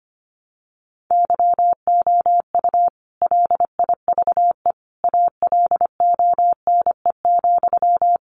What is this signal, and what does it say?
Morse code, heard as a keyed tone, about 25 wpm, 700 Hz. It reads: YOU LIVE ALONE,
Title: Sentences from Top 1000 Words 25wpm